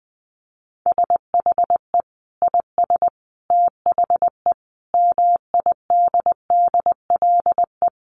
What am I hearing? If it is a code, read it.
Morse code, heard as a keyed tone, about 20 wpm, 700 Hz. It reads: SHE IS THE MIDDLE
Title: Sentences from Top 900 Words 20wpm